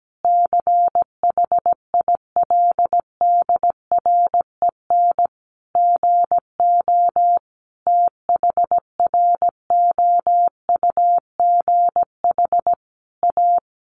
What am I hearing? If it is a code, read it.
Morse code, heard as a keyed tone, about 17 wpm, 700 Hz. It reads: CHILDREN GO THROUGH A
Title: Sentences from Top 1200 Words 17wpm